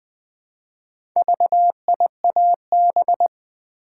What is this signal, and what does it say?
Morse code, heard as a keyed tone, about 20 wpm, 700 Hz. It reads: VIAB
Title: Build A Word - 2 Letters At A Time - 6-Letter Words - 20wpm